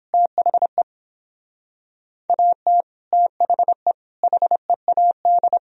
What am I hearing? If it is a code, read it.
Morse code, heard as a keyed tone, about 29 wpm, 700 Hz. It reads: THE AT THE HEAD